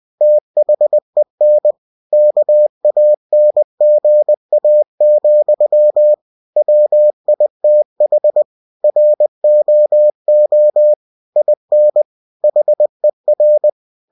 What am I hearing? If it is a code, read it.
Morse code, heard as a keyed tone, about 20 wpm, 600 Hz. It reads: THEN KANGA, WITH ROO IN HER
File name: cw-books / winnie-the-pooh / winnie-20wpm-page-0079